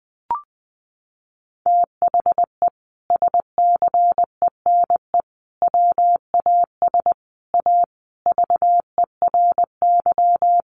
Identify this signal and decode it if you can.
Morse code, heard as a keyed tone, about 19 wpm, 700 Hz. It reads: THE SCENE WAS A VERY
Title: Sentences from Top 1100 Words 20wpm